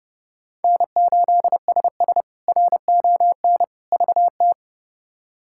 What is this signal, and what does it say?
Morse code, heard as a keyed tone, about 32 wpm, 700 Hz. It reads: N8SS RON VT
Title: K1USN SST QSO Practice - 30wpm